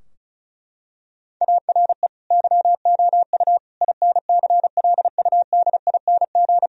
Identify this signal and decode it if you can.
Morse code, heard as a keyed tone, about 36 wpm, 700 Hz. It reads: ARE YOU INCLUDING
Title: Sentences from Top 1000 Words - Review Part 4 - 35wpm